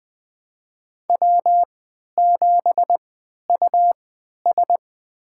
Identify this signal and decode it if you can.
Morse code, heard as a keyed tone, about 20 wpm, spaced as for 11 wpm, 700 Hz. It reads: W7US
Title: USA Call Signs - 4 Characters - 3x Character Spacing - 20wpm